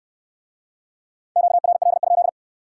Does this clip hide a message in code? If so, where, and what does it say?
Yes: Morse code from 1.4 to 2.3 s, heard as a keyed tone, about 69 wpm, 700 Hz. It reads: QRL?